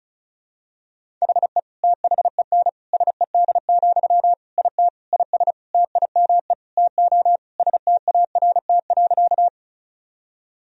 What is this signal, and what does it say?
Morse code, heard as a keyed tone, about 36 wpm, 700 Hz. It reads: HE THEN SED, IT IS TIME TO START.
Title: Sentences from Top 300 Words and Abbreviations 35wpm